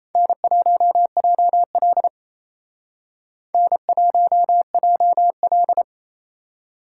Morse code, heard as a keyed tone, about 35 wpm, 700 Hz. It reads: N1JL N1JL
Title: Speed Racing - USA 1x2 Call Signs 28wpm